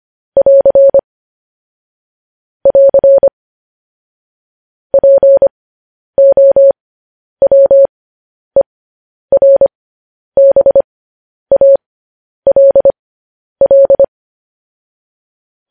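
Morse code, heard as a keyed tone, about 25 wpm, spaced as for 10 wpm, 550 Hz. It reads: <AR> <AR> POWERBALL
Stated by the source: Morse code news headlines sent at 10 wpm for practice.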